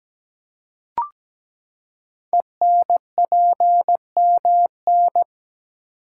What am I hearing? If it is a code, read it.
Morse code, heard as a keyed tone, about 17 wpm, 700 Hz. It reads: ENPMN
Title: Code Groups - 5 Letter:Numbers - 17wpm